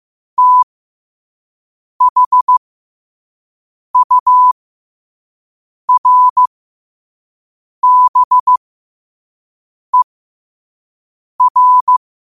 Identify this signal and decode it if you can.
Morse code, heard as a keyed tone, about 15 wpm, spaced as for 5 wpm, 1000 Hz. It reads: THURBER
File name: 2026-03-16-5wpm